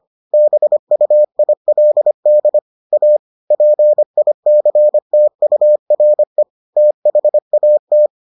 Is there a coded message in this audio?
Yes: Morse code from 0.3 to 8.1 s, heard as a keyed tone, about 25 wpm, 600 Hz. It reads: BUILD A PICTURE THAT